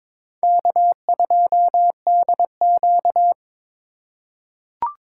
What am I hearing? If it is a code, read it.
Morse code, heard as a keyed tone, about 22 wpm, 700 Hz. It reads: K2DQ
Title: USA Call Signs - 4 Characters - 2x Character Spacing - 22wpm